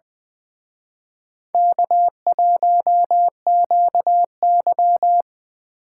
Morse code, heard as a keyed tone, about 23 wpm, 700 Hz. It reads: K1QY
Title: Speed Racing - USA 1x2 Call Signs 20wpm